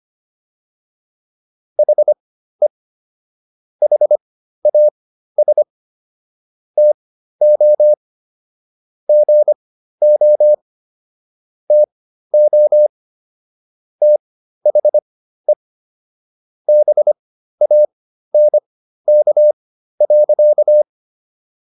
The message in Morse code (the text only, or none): HE HAS TO GO TO THE BANK.